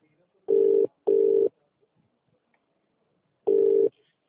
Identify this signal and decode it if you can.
Ringback tone (UK), 400 + 450 Hz, with speech.